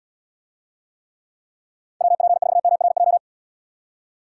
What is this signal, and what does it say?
Morse code, heard as a keyed tone, about 79 wpm, 700 Hz. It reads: WP4MRJ